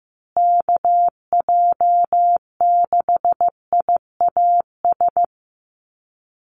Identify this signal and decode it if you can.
Morse code, heard as a keyed tone, about 15 wpm, 700 Hz. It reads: KJ6IAS
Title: USA Call Signs Repeated 2x 15wpm